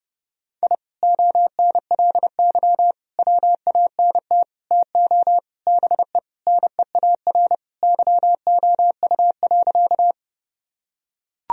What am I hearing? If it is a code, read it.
Morse code, heard as a keyed tone, about 31 wpm, 700 Hz. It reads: I ONLY WANT TO BE NEAR YOU.
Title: Sentences from Top 1000 Words - Review Part 1 - 30wpm